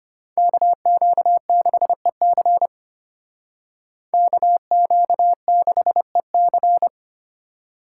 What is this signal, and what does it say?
Morse code, heard as a keyed tone, about 32 wpm, 700 Hz. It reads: KQ6EC KQ6EC